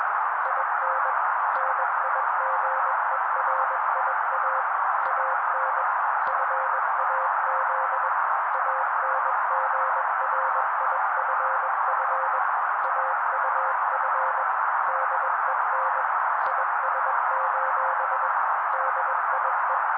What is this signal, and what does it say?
Morse code, heard as a keyed tone, about 21 wpm, 530 Hz. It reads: IN NIGERIA AN FAZ ANGRIFF AUF DEN IS: DIE